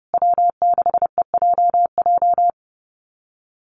Morse code, heard as a keyed tone, about 30 wpm, 700 Hz. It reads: W6EJJ
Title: USA Call Signs Repeated 3x 30wpm